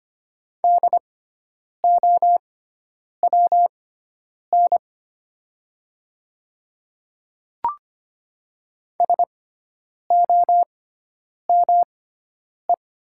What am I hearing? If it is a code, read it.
Morse code, heard as a keyed tone, about 25 wpm, spaced as for 9 wpm, 700 Hz. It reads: DOWN SOME
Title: 6x Character Spacing - Top 100 Words - 25wpm